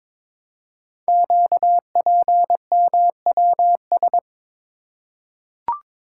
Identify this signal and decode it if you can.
Morse code, heard as a keyed tone, about 22 wpm, 700 Hz. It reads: QPMWS
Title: Code Groups - 5 Letters - 22wpm